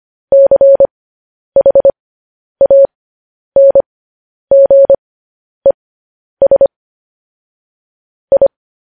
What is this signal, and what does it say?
Morse code, heard as a keyed tone, about 25 wpm, spaced as for 10 wpm, 550 Hz. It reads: CHANGES I
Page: News Headlines in Morse Code at 10 WPM